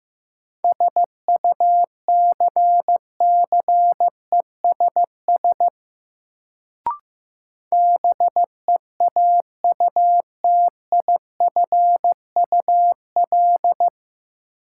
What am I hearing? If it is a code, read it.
Morse code, heard as a keyed tone, about 15 wpm, 700 Hz. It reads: SUCCESS BEAUTIFUL
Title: Sentences from Top 900 Words - New Word Drill - 15wpm